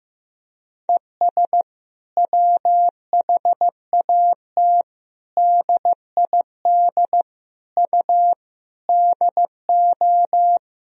Morse code, heard as a keyed tone, about 15 wpm, 700 Hz. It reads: ES WHAT DID U DO
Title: Sentences from Top 300 Words and Abbreviations 15wpm